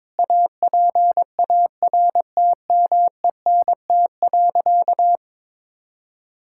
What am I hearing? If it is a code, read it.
Morse code, heard as a keyed tone, about 22 wpm, 700 Hz. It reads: APARTMENT.
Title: Sentences from Top 1000 Words - Review Part 6 - 22wpm